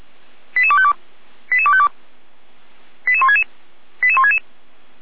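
Five-tone selective call (ZVEI), heard as quick runs of five tones, tones of about 70 ms. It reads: 88252 88252 88166 88166
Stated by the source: ZVEI five-tone selective call received by SDR.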